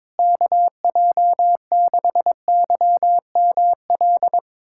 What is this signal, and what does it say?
Morse code, heard as a keyed tone, about 22 wpm, 700 Hz. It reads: KJ6YML
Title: USA Callsign DE Callsign - 6 Characters - 22wpm